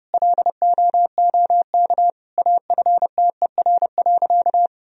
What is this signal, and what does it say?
Morse code, heard as a keyed tone, about 30 wpm, 700 Hz. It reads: LOOK AFTER.
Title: Sentences from Top 1000 Words - Review Part 3 - 30wpm